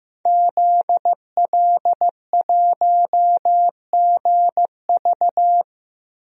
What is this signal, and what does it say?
Morse code, heard as a keyed tone, about 15 wpm, 700 Hz. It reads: ZL1GV
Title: DXCC-100 Call Signs - 5 Characters - 15wpm